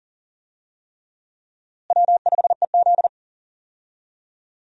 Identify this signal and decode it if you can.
Morse code, heard as a keyed tone, about 40 wpm, 700 Hz. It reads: W5EZ
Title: USA Call Signs - 4 Characters - Repeated 3x - 40wpm